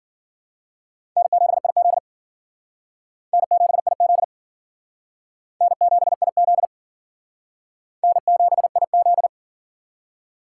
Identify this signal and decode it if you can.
Morse code, heard as a keyed tone, about 50 wpm, 700 Hz. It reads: N7IZ N7IZ N7IZ N7IZ